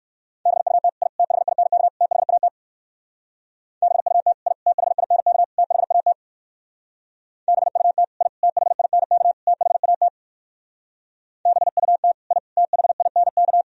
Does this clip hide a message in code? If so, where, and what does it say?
Yes: Morse code from 0.5 to 13.6 s, heard as a keyed tone, about 58 wpm, 700 Hz. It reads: BUT I THINK THAT BUT I THINK THAT BUT I THINK THAT BUT I THINK